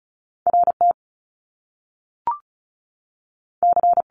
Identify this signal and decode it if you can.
Morse code, heard as a keyed tone, about 36 wpm, 700 Hz. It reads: R T C
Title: Top Two Letter Combinations in Words 35wpm